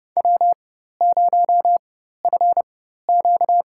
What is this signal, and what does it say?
Morse code, heard as a keyed tone, about 32 wpm, spaced as for 14 wpm, 700 Hz. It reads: W0FQ
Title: USA Call Signs - 4x Character Spacing - 30wpm